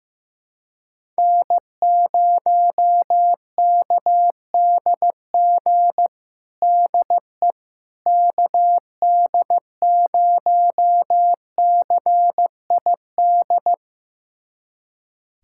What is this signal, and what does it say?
Morse code, heard as a keyed tone, about 15 wpm, 700 Hz. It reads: N0KDG DE KD0CID
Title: USA Callsign DE Callsign 15wpm